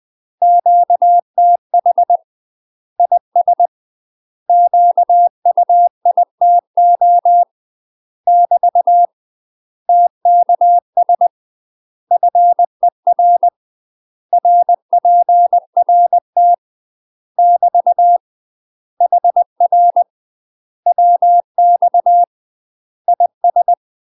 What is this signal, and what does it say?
Morse code, heard as a keyed tone, about 20 wpm, 700 Hz. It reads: QTH IS QUITO <BT> TKS FER RPRT <BT> HR WX IS
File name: UK MASTER QSOs TRAINING / 20WPM / UK MASTER QSOs 0073